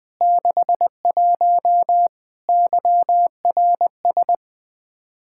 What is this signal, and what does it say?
Morse code, heard as a keyed tone, about 20 wpm, 700 Hz. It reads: 61 YRS
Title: Practice QSO - Exchange 3 - More Info - 20wpm